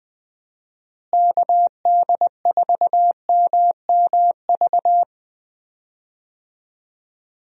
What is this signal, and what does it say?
Morse code, heard as a keyed tone, about 20 wpm, 700 Hz. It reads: KD4MMV
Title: USA Call Signs - 6 Characters - Repeated 3x - 20wpm